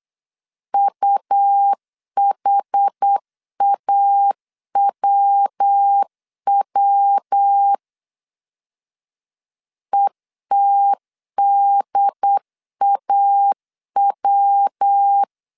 Morse code, heard as a keyed tone, about 8 wpm, 800 Hz. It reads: UHAWW ETDAW